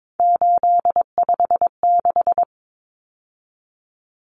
Morse code, heard as a keyed tone, about 22 wpm, 700 Hz. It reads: 856
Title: Numbers - 3-Digit 22wpm